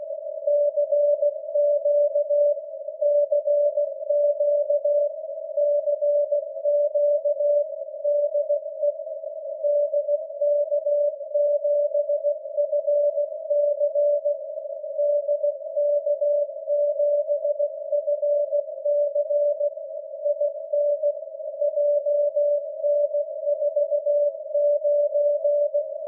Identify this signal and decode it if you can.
Morse code, heard as a keyed tone, about 16 wpm, 580 Hz. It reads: CQ CQ CQ DE DK7FC DK7FC IN JN49
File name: dk7fc-472kHz